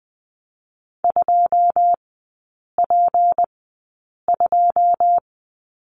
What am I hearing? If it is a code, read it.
Morse code, heard as a keyed tone, about 20 wpm, spaced as for 8 wpm, 690 Hz. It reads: 2P2
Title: 159 Lesson 26e - Letter g - Three Characters 20wpm